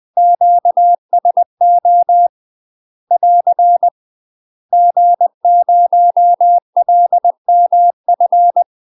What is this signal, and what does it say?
Morse code, heard as a keyed tone, about 20 wpm, 700 Hz. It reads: QSO <AR> G0LMF
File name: UK MASTER QSOs TRAINING / 20WPM / UK MASTER QSOs 0059